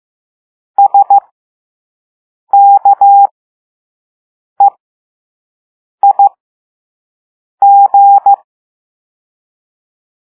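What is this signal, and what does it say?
Morse code, heard as a keyed tone, about 15 wpm, spaced as for 5 wpm, 800 Hz. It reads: SKEIG
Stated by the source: Morse code lesson 5 drill of random five-letter groups using all letters learned so far.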